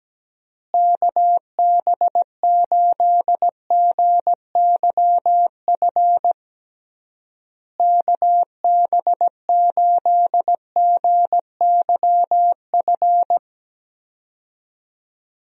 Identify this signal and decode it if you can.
Morse code, heard as a keyed tone, about 17 wpm, 700 Hz. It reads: KB8GYF KB8GYF
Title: USA Call Signs - 6 Characters - Repeated 3x - 17wpm